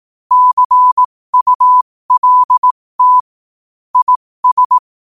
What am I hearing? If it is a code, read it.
Morse code, heard as a keyed tone, about 18 wpm, spaced as for 15 wpm, 1000 Hz. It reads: CULT IS
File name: qotdfiles / 15wpm